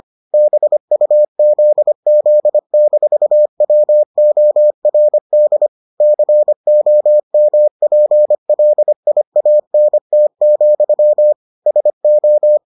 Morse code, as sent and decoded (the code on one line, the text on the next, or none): -... ..- --.. --.. -....- .-- --- .-. -.. / -.-. --- -- .--. .-.. .. .- -. - --..-- / ... ---
BUZZ-WORD COMPLIANT, SO